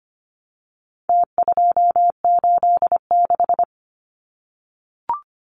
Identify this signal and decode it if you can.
Morse code, heard as a keyed tone, about 25 wpm, 700 Hz. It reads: T286
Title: Numbers - Contest Style - 4-Digit 25wpm